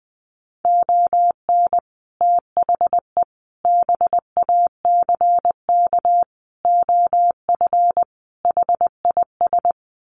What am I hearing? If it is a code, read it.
Morse code, heard as a keyed tone, about 20 wpm, 700 Hz. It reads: ON THE BACK OF HIS
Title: Sets of 5 Words Encore 20wpm